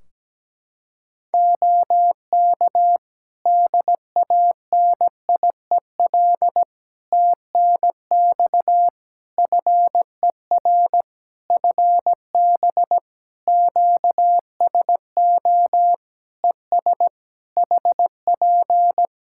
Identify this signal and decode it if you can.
Morse code, heard as a keyed tone, about 17 wpm, 700 Hz. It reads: OK DANIEL TNX FER FB QSO ES HP
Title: Practice QSO - Exchange 4 - Ending - 17wpm